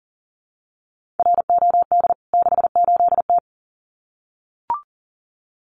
Morse code, heard as a keyed tone, about 40 wpm, 700 Hz. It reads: ROD 68T
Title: CWT Exchange 40wpm